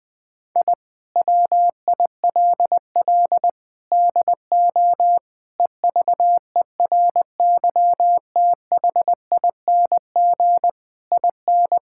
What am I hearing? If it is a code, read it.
Morse code, heard as a keyed tone, about 20 wpm, 700 Hz. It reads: I WILL DO EVERYTHING IN
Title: Sets of 8 Words - 20wpm